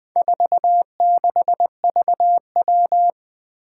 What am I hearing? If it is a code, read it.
Morse code, heard as a keyed tone, about 21 wpm, 700 Hz. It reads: 46VW
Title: Four Challenging Letter-Number Combinations 20wpm